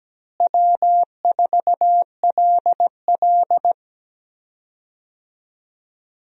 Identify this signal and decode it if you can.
Morse code, heard as a keyed tone, about 17 wpm, 700 Hz. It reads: W4LL
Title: USA Call Signs - 4 Characters - Repeated 3x - 17wpm